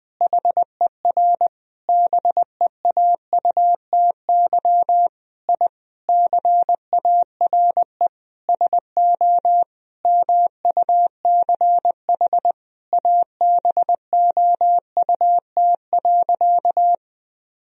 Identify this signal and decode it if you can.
Morse code, heard as a keyed tone, about 20 wpm, 700 Hz. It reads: HER BEAUTY I CARE SO MUCH ABOUT.
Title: Sentences from Top 1000 Words - Review Part 2 - 20wpm